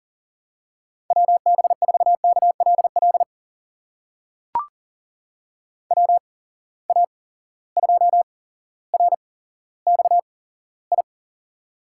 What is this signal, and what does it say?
Morse code, heard as a keyed tone, about 40 wpm, spaced as for 17 wpm, 700 Hz. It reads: WB4KLL WA2RXI
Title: USA Call Signs - 6 Characters - 8x Character Spacing - 40wpm